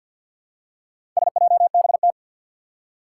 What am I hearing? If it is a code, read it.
Morse code, heard as a keyed tone, about 51 wpm, 700 Hz. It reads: SJBT
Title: Four Challenging Letter-Number Combinations 50wpm